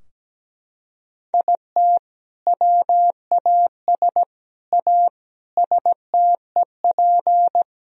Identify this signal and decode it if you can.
Morse code, heard as a keyed tone, about 17 wpm, 700 Hz. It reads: IT WAS A STEP
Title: Sentences from Top 1000 Words - Review Part 3 - 17wpm